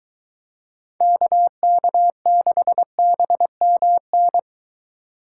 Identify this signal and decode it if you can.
Morse code, heard as a keyed tone, about 23 wpm, 700 Hz. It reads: KK6BMN